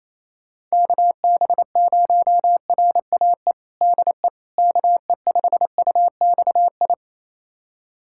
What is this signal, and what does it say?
Morse code, heard as a keyed tone, about 28 wpm, 700 Hz. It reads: KB0RAE DE KE5UXI
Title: USA Callsign DE Callsign 28wpm